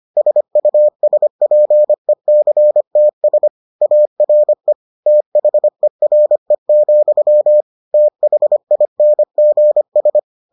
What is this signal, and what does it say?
Morse code, heard as a keyed tone, about 25 wpm, 600 Hz. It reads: SUSPECTS ARE THERE, THINGS